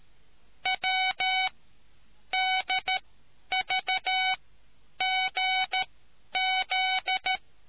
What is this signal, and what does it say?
Morse code, heard as a keyed tone, about 13 wpm, spaced as for 9 wpm, 750 Hz. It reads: WDVGZ